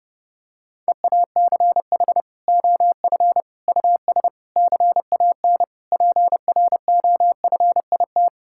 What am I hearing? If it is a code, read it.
Morse code, heard as a keyed tone, about 30 wpm, 700 Hz. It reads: EACH OF US CAN PROFIT
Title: Sentences from Top 1000 Words - Review Part 5 - 30wpm